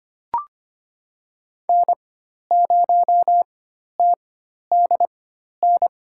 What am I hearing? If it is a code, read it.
Morse code, heard as a keyed tone, about 25 wpm, spaced as for 12 wpm, 700 Hz. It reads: N0TDN
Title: USA Call Signs - 5 Characters - 4x Character Spacing - 25wpm